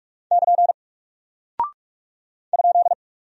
Morse code, heard as a keyed tone, about 44 wpm, 700 Hz. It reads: <KN> ?
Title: Single Character (Letter:Number:Punctuation:Prosign) 45wpm